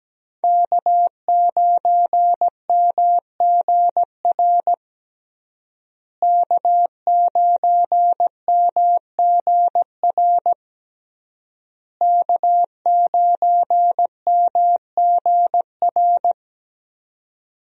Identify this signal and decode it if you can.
Morse code, heard as a keyed tone, about 17 wpm, 700 Hz. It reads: K9MGR K9MGR K9MGR
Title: USA Call Signs - 5 Characters - Repeated 3x - 17wpm